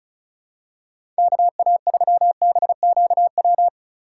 Morse code, heard as a keyed tone, about 34 wpm, 700 Hz. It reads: KA3BQW
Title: USA Call Signs - 6 Characters - 6x Character Spacing - 35wpm